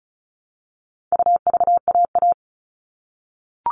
Morse code, heard as a keyed tone, about 34 wpm, 700 Hz. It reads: UVAA
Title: All Cut Numbers - 4-Digit 35wpm